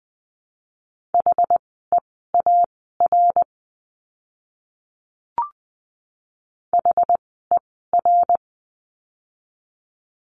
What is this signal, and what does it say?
Morse code, heard as a keyed tone, about 20 wpm, spaced as for 14 wpm, 700 Hz. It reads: HEAR HER
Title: 206 Lesson 33f - Letter j - Words 20wpm